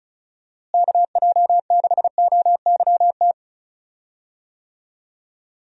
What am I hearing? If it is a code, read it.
Morse code, heard as a keyed tone, about 36 wpm, 700 Hz. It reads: KJ6OYT
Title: USA Call Signs - 6 Characters - Repeated 2x - 35wpm